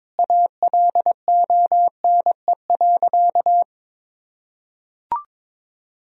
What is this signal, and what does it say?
Morse code, heard as a keyed tone, about 22 wpm, 700 Hz. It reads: ALONE.
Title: Sentences from Top 1000 Words - Review Part 2 - 22wpm